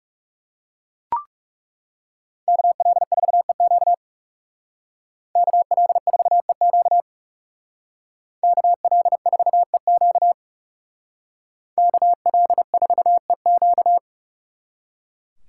Morse code, heard as a keyed tone, about 37 wpm, 700 Hz. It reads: KL4EQ KL4EQ KL4EQ KL4EQ
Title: Speed Racing - USA 2x2 Call Signs 30wpm